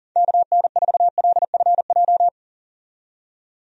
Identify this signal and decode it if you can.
Morse code, heard as a keyed tone, about 40 wpm, 690 Hz. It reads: KN4LFJ
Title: USA Call Signs - 6 Characters - 40wpm